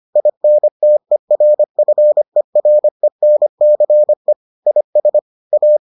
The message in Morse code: INTERFERENCE IS A

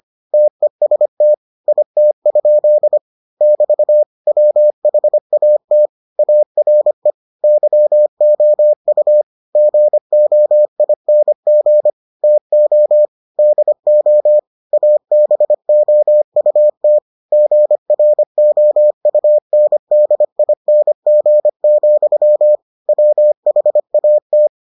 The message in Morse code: TEST IT? <BT> WHAT ARE YOU GOING TO DO ABOUT GROUNDING, WHAT

